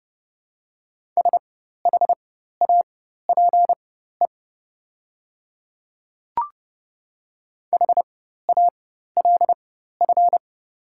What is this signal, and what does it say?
Morse code, heard as a keyed tone, about 30 wpm, spaced as for 14 wpm, 700 Hz. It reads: SHAPE HALF
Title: 4x Character Spacing - Top 500 Words - 30wpm